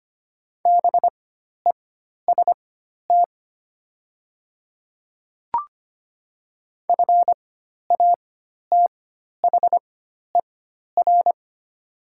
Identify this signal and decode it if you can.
Morse code, heard as a keyed tone, about 25 wpm, spaced as for 12 wpm, 700 Hz. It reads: BEST FATHER
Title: 4x Character Spacing - Top 500 Words - 25wpm